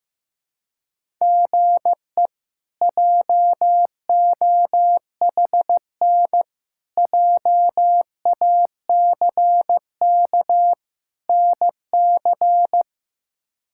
Morse code, heard as a keyed tone, about 15 wpm, 700 Hz. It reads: GE JOHN JACK NC
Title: K1USN SST QSO Practice - 15wpm